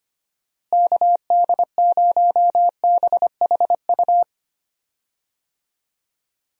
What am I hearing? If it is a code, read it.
Morse code, heard as a keyed tone, about 25 wpm, 700 Hz. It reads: KD0BHU
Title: USA Call Signs - 6 Characters - Repeated 2x - 25wpm